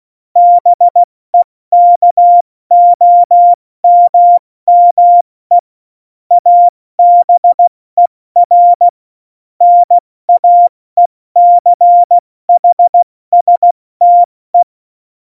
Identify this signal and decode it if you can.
Morse code, heard as a keyed tone, about 16 wpm, 700 Hz. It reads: BEKOMME ABER NAECHSTE
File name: Morsekurs / Witze / Witz 02 Juni 2020, 70 F80